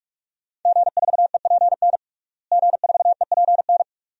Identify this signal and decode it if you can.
Morse code, heard as a keyed tone, about 46 wpm, 700 Hz. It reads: G4EPN G4EPN
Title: DXCC-100 Call Signs - 5 Characters - Repeated 2x - 45wpm